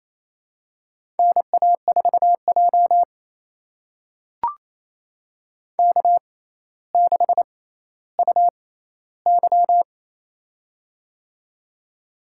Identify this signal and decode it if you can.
Morse code, heard as a keyed tone, about 28 wpm, spaced as for 14 wpm, 700 Hz. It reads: NA4J K6UY
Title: USA Call Signs - 4 Characters - 6x Character Spacing - 28wpm